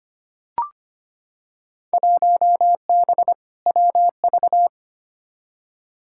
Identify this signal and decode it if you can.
Morse code, heard as a keyed tone, about 25 wpm, 700 Hz. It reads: 1B WV
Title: Field Day Exchanges - 25wpm